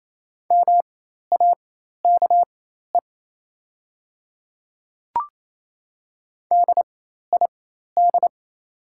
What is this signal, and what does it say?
Morse code, heard as a keyed tone, about 28 wpm, spaced as for 13 wpm, 690 Hz. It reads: MAKE DID
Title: 4x Character Spacing - Top 200 Words - 28wpm